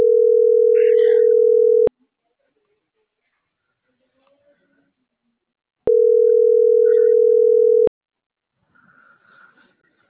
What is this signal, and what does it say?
Ringback tone (North America), 440 + 480 Hz.